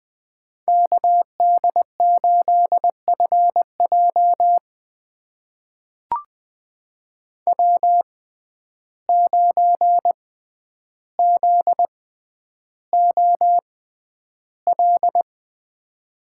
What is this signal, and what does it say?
Morse code, heard as a keyed tone, about 20 wpm, spaced as for 10 wpm, 700 Hz. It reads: KD8FJ W9ZOL
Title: USA Call Signs - 5 Characters - 6x Character Spacing - 20wpm